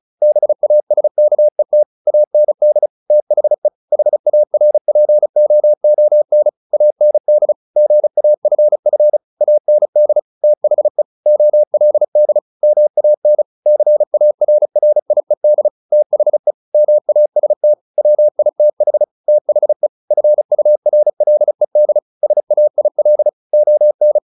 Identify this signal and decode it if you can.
Morse code, heard as a keyed tone, about 34 wpm, 600 Hz. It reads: BASKET AND THE HARPOON AND GAFF AND THE OLD MAN CARRIED THE MAST WITH THE FURLED SAIL ON
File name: the old man and the sea / the old man and the sea 1 35wpm